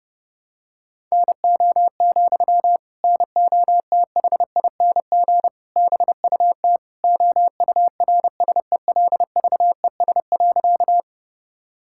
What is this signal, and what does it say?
Morse code, heard as a keyed tone, about 30 wpm, 700 Hz. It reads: NO, NOTHING BUT OURSELVES.
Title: Sentences from Top 1600 Words 30wpm